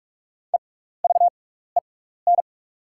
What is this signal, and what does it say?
Morse code, heard as a keyed tone, about 44 wpm, spaced as for 15 wpm, 700 Hz. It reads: EVEN
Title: 6x Character Spacing - Top 500 Words - 45wpm